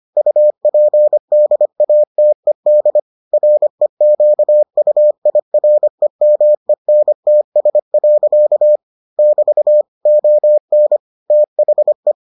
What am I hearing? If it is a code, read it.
Morse code, heard as a keyed tone, about 25 wpm, 600 Hz. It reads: UPDATED REQUIREMENTS. <BT> ON THE